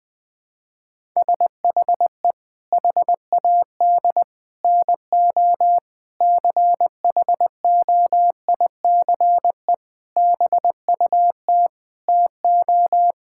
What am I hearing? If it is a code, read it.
Morse code, heard as a keyed tone, about 21 wpm, 700 Hz. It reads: SHE HAD NO CHOICE BUT TO
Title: Sentences with 5-Word Phrases - 20wpm